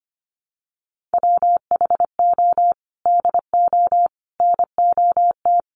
Morse code, heard as a keyed tone, about 25 wpm, 700 Hz. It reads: WHO DO NOT
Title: Sets of 5 Words Encore2 25wpm